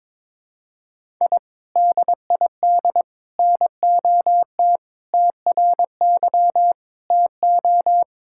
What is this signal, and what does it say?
Morse code, heard as a keyed tone, about 22 wpm, 700 Hz. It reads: I DID NOT TRY TO
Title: Sentences from Top 1000 Words - Review Part 5 - 22wpm